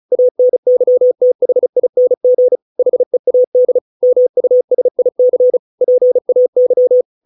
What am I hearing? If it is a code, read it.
Morse code, heard as a keyed tone, about 34 wpm, 500 Hz. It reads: ANYTHING HEAD MUSIC PAY